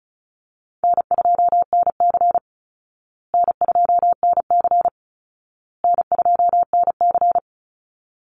Morse code, heard as a keyed tone, about 34 wpm, 700 Hz. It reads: N2NC N2NC N2NC
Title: Easy Call Signs Repeated 3x - 35wpm